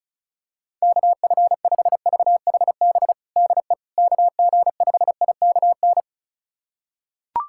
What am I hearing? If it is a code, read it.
Morse code, heard as a keyed tone, about 36 wpm, 700 Hz. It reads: KF5VHB DE KG5IKN
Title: USA Callsign DE Callsign - 6 Characters - 35wpm